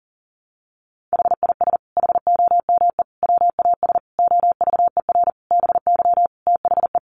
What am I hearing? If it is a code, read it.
Morse code, heard as a keyed tone, about 40 wpm, 700 Hz. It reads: HIS HOME WAS OVER BY THE
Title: Sentences from Top 1500 Words 40wpm